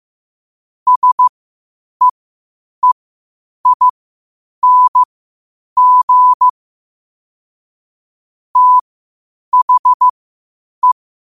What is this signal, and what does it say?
Morse code, heard as a keyed tone, about 15 wpm, spaced as for 8 wpm, 1000 Hz. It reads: SEEING THE
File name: qotdfiles / 8wpm